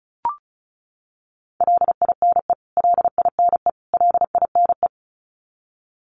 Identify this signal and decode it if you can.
Morse code, heard as a keyed tone, about 34 wpm, 700 Hz. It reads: LINE LINE LINE
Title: Top 200 Words Repeated 35wpm